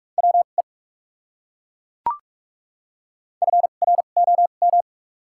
Morse code, heard as a keyed tone, about 42 wpm, spaced as for 31 wpm, 700 Hz. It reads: WE FROM
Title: 2x Character Spacing - Top 100 Words - 45wpm